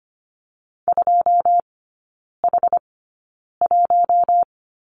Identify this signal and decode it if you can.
Morse code, heard as a keyed tone, about 25 wpm, spaced as for 9 wpm, 700 Hz. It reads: 2H1
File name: Lesson 17e - Letter u - Three Characters 25wpm